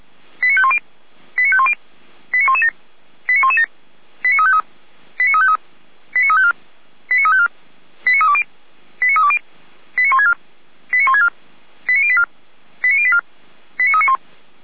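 Five-tone selective call (ZVEI), heard as quick runs of five tones, tones of about 70 ms. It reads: 89510 89510 89117 89117 89353 89353 89354 89354 89320 89320 89164 89164 89084 89084 89391
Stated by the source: ZVEI five-tone call 89126, received on SDR.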